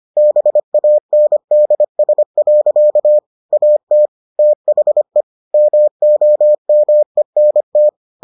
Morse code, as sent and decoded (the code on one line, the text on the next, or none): -... .- -. -.. ... .-.-.- / .- - / - .... / -- --- -- . -. -
BANDS. AT THE MOMENT